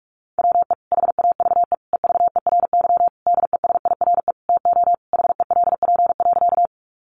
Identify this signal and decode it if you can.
Morse code, heard as a keyed tone, about 44 wpm, 700 Hz. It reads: WE HAVE EVERY DESIRE TO HELP.
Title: Sentences from Top 1100 Words 45wpm